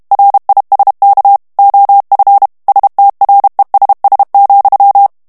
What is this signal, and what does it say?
Morse code, heard as a keyed tone, about 32 wpm, 800 Hz. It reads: RISK OF STRESS,
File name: morse mp3 / ANGLAIS texte clair 1 vitesse 30 mots minute